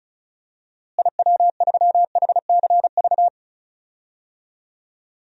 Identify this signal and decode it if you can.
Morse code, heard as a keyed tone, about 34 wpm, 700 Hz. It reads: IW3HCV
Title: DXCC-100 Call Signs - 6 Characters - Repeated 2x - 35wpm